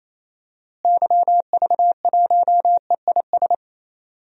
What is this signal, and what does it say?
Morse code, heard as a keyed tone, about 27 wpm, 700 Hz. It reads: YV1EIS
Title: DXCC-100 Call Signs - 6 Characters - Repeated 3x - 28wpm